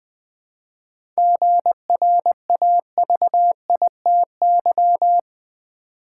Morse code, heard as a keyed tone, about 20 wpm, 700 Hz. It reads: GRAVITY
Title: 7 Letter Words - First 3 Letters Given - 20wpm